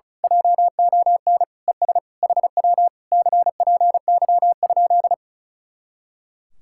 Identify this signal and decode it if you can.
Morse code, heard as a keyed tone, about 33 wpm, 700 Hz. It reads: JON ES HW CPY?
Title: Speed-Racing Common QSO Phrases - 35wpm